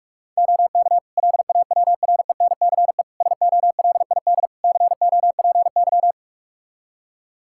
Morse code, heard as a keyed tone, about 41 wpm, 700 Hz. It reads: OK LAWRENCE SOLID COPY
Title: Practice QSO - Exchange 3 - More Info - 45wpm